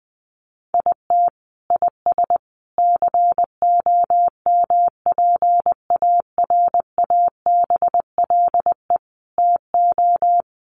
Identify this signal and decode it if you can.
Morse code, heard as a keyed tone, about 20 wpm, 700 Hz. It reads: IT IS COMPARABLE TO
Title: A Sound Of Thunder 20wpm